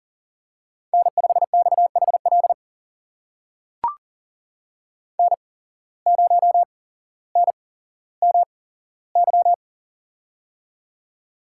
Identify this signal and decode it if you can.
Morse code, heard as a keyed tone, about 43 wpm, spaced as for 17 wpm, 700 Hz. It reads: N5XHL N0NMY
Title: USA Call Signs - 5 Characters - 8x Character Spacing - 40wpm